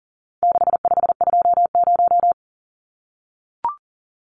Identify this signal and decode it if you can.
Morse code, heard as a keyed tone, about 40 wpm, 700 Hz. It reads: 6520
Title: Numbers - 4-Digit 40wpm